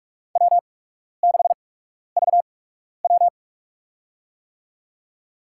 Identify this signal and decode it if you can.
Morse code, heard as a keyed tone, about 46 wpm, spaced as for 12 wpm, 700 Hz. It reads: W6VW
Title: USA Call Signs - 4 Characters - 8x Character Spacing - 45wpm